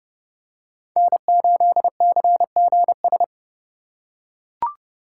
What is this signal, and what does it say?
Morse code, heard as a keyed tone, about 30 wpm, 700 Hz. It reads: N8CGS